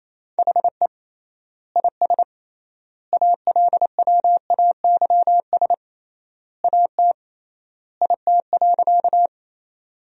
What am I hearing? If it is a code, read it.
Morse code, heard as a keyed tone, about 27 wpm, 700 Hz. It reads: HE IS ALWAYS AT IT.
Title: 3x Word-Spacing - Sentences from Top 300 Words - 28wpm